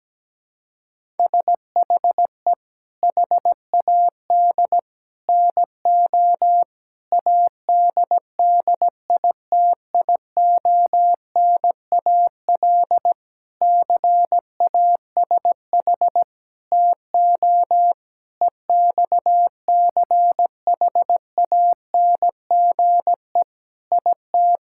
Morse code, heard as a keyed tone, about 17 wpm, 700 Hz. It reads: SHE HAD NO ADDITIONAL CASH TO EXCHANGE IT
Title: Sentences from Top 1000 Words - Review Part 6 - 17wpm